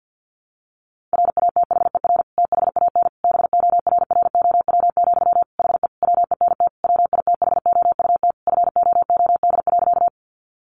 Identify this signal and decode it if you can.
Morse code, heard as a keyed tone, about 50 wpm, 700 Hz. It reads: RATHER THAN BORROW, HE WENT WITHOUT FOOD.
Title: Sentences from Top 1500 Words 50wpm